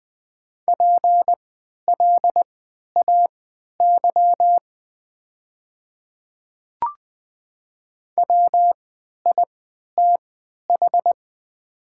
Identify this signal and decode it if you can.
Morse code, heard as a keyed tone, about 20 wpm, spaced as for 11 wpm, 690 Hz. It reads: PLAY WITH
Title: 3x Character Spacing - Top 200 Words - 20wpm